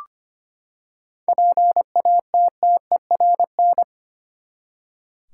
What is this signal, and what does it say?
Morse code, heard as a keyed tone, about 25 wpm, 700 Hz. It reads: PATTERN
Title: Seven Letter Words - 25wpm